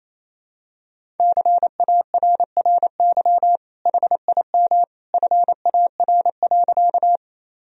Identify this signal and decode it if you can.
Morse code, heard as a keyed tone, about 27 wpm, 700 Hz. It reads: CARRY HIM FAR.
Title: Sentences from Top 1000 Words - Review Part 1 - 28wpm